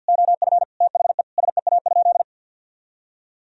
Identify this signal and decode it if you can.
Morse code, heard as a keyed tone, about 50 wpm, 700 Hz. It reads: OF THE SEA?
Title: Sentences from Top 1000 Words - Review Part 3 - 50wpm